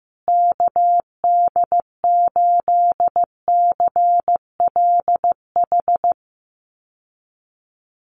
Morse code, heard as a keyed tone, about 15 wpm, 700 Hz. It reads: KD8CLH
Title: USA Call Signs Repeated 2x 15wpm